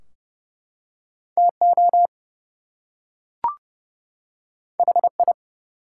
Morse code, heard as a keyed tone, about 30 wpm, 700 Hz. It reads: TO HI
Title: Two Letter Words - 30wpm